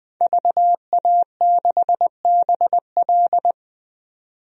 Morse code, heard as a keyed tone, about 20 wpm, 700 Hz. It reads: VA6BL
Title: DXCC-100 Call Signs - 5 Characters - 20wpm